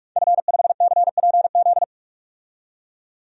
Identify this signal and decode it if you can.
Morse code, heard as a keyed tone, about 41 wpm, 700 Hz. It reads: F5CPZ
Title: DXCC-100 Call Signs - 5 Characters - 45wpm